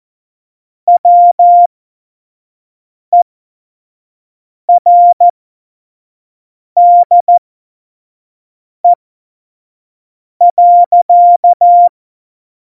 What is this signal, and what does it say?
Morse code, heard as a keyed tone, about 14 wpm, spaced as for 5 wpm, 700 Hz. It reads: WERDE.
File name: Morsekurs / Witze / Witz 05. Dezember 2017, 25 F70